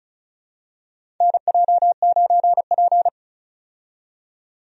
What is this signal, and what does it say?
Morse code, heard as a keyed tone, about 36 wpm, 700 Hz. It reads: NJ9P
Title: USA Call Signs - 4 Characters - 35wpm